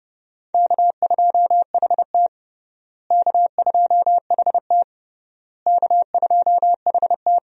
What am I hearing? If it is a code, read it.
Morse code, heard as a keyed tone, about 30 wpm, 700 Hz. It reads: K2HT K2HT K2HT
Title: USA Call Signs - 4 Characters - Repeated 3x - 30wpm